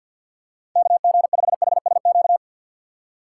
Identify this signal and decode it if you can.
Morse code, heard as a keyed tone, about 47 wpm, 700 Hz. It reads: KG5HSQ
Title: USA Call Signs - 6 Characters - 4x Character Spacing - 50wpm